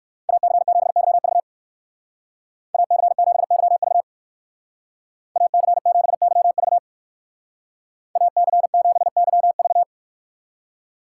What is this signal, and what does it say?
Morse code, heard as a keyed tone, about 59 wpm, 700 Hz. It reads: AC7YV AC7YV AC7YV AC7YV